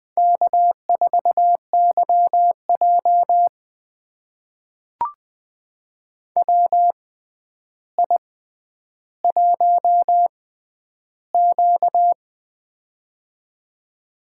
Morse code, heard as a keyed tone, about 21 wpm, spaced as for 10 wpm, 700 Hz. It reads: K4YJ WI1Q
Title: USA Call Signs - 4 Characters - 6x Character Spacing - 20wpm